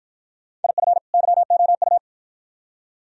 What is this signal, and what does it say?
Morse code, heard as a keyed tone, about 53 wpm, 700 Hz. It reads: IF YOU